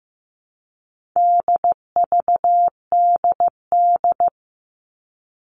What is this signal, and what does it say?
Morse code, heard as a keyed tone, about 15 wpm, 700 Hz. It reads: DVDD
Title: All Cut Numbers - 4-Digit 15wpm